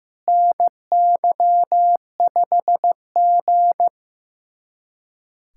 Morse code, heard as a keyed tone, about 15 wpm, 700 Hz. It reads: NY5G
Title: USA Callsign DE Callsign - 4 Characters - 15wpm